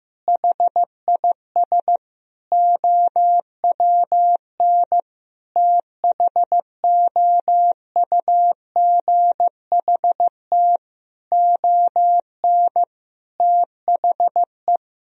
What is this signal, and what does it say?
Morse code, heard as a keyed tone, about 15 wpm, 700 Hz. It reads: HIS OWN THOUGHT ON THE
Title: Sentences from Top 1000 Words - Review Part 1 - 15wpm